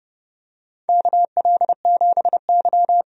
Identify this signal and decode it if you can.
Morse code, heard as a keyed tone, about 31 wpm, 700 Hz. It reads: KL7Y
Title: DXCC-100 Call Signs - 4 Characters - 30wpm